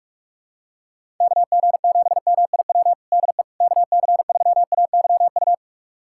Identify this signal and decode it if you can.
Morse code, heard as a keyed tone, about 46 wpm, 700 Hz. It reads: KG7MIW DE KC3AYU
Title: USA Callsign DE Callsign 45wpm